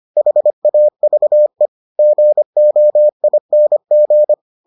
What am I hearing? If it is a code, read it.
Morse code, heard as a keyed tone, about 25 wpm, 600 Hz. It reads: HAVE GOING